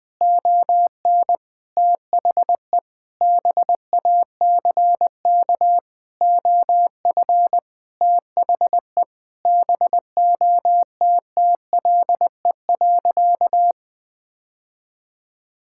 Morse code, heard as a keyed tone, about 20 wpm, 700 Hz. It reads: ON THE BACK OF THE BOTTLE.
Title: Sentences with 4-Word Phrases - Spoken Beginning - 20wpm